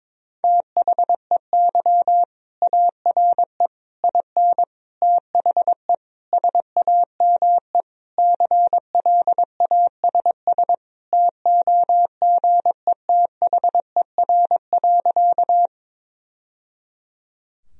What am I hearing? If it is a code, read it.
Morse code, heard as a keyed tone, about 22 wpm, 700 Hz. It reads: THEY ARE IN THE SAME CLASS TOGETHER.
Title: Sentences from Top 1000 Words - Review Part 2 - 22wpm